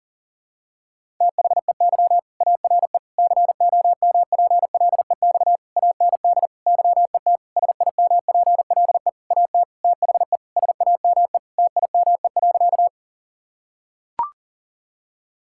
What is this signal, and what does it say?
Morse code, heard as a keyed tone, about 40 wpm, 700 Hz. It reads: THEY ARE COMPLEX AND YET SIMPLE AT THE SAME TIME.
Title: Sentences from Top 1000 Words - Review Part 6 - 40wpm